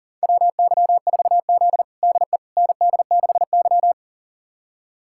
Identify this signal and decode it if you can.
Morse code, heard as a keyed tone, about 40 wpm, 700 Hz. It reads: WY4Z DE ND6Y
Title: USA Callsign DE Callsign - 4 Characters - 40wpm